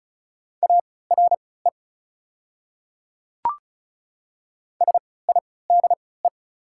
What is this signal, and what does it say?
Morse code, heard as a keyed tone, about 33 wpm, spaced as for 20 wpm, 690 Hz. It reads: ARE SIDE
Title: 3x Character Spacing - Top 100 Words - 35wpm